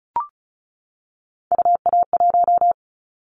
Morse code, heard as a keyed tone, about 36 wpm, 700 Hz. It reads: UA1
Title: Callsign Prefixes for Top 25 DXCC - 35wpm